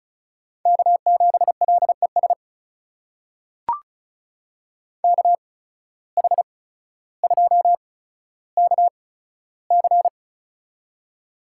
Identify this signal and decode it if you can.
Morse code, heard as a keyed tone, about 36 wpm, spaced as for 15 wpm, 700 Hz. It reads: K7LES KH2KC
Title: USA Call Signs - 5 Characters - 8x Character Spacing - 35wpm